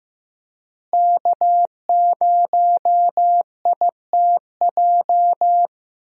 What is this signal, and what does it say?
Morse code, heard as a keyed tone, about 15 wpm, 700 Hz. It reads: K0ITJ
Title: USA Call Signs - 5 Characters - Repeated 3x - 15wpm